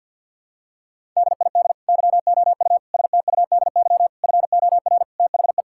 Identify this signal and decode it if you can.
Morse code, heard as a keyed tone, about 50 wpm, 700 Hz. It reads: DID YOU STUDY FOR THE
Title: Sentences from Top 1000 Words - Review Part 5 - 50wpm